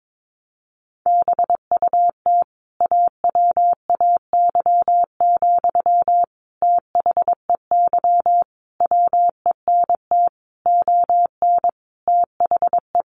Morse code, heard as a keyed tone, about 22 wpm, 700 Hz. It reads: BUT AWAY, THEY WENT ON THE
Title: Sentences from Top 1400 Words 22wpm